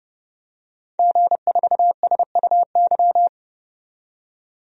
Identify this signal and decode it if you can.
Morse code, heard as a keyed tone, about 31 wpm, 700 Hz. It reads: G4SUY
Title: DXCC-100 Call Signs - 5 Characters - 30wpm